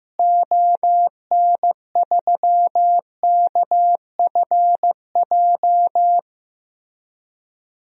Morse code, heard as a keyed tone, about 15 wpm, 700 Hz. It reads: ON3KFJ
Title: DXCC-100 Call Signs - 6 Characters - Repeated 3x - 15wpm